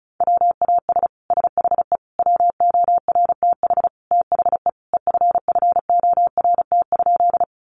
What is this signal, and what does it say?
Morse code, heard as a keyed tone, about 34 wpm, 700 Hz. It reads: WAS SHE WORTH THE EFFORT?
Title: Sentences from Top 1200 Words 35wpm